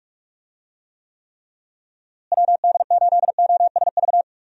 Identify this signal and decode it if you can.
Morse code, heard as a keyed tone, about 46 wpm, 700 Hz. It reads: WD8OSV
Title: USA Call Signs - 6 Characters - Repeated 3x - 45wpm